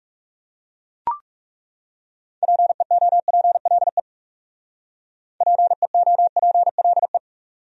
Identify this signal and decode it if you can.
Morse code, heard as a keyed tone, about 40 wpm, 700 Hz. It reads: PEOPLE PEOPLE